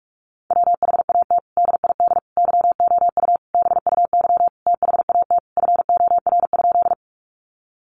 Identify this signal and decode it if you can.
Morse code, heard as a keyed tone, about 44 wpm, 700 Hz. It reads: WHAT DID YOU BUY THAT FOR?
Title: Sentences from Top 700 Words 45wpm